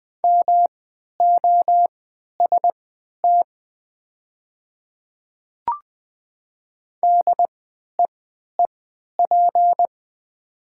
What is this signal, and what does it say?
Morse code, heard as a keyed tone, about 20 wpm, spaced as for 11 wpm, 700 Hz. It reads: MOST DEEP
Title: 3x Character Spacing - Top 500 Words - 20wpm